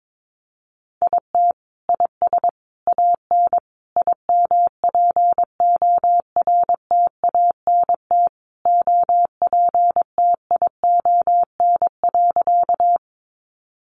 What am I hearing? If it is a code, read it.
Morse code, heard as a keyed tone, about 22 wpm, 700 Hz. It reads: IT IS AN IMPORTANT OPTION.
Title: Sentences from Top 1300 Words 22wpm